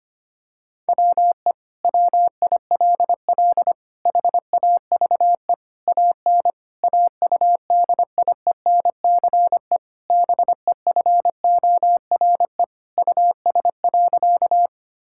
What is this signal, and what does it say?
Morse code, heard as a keyed tone, about 25 wpm, 700 Hz. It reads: WE WILL HAVE AN AUDIENCE BEFORE US.
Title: Sentences from Top 1000 Words - Review Part 5 - 25wpm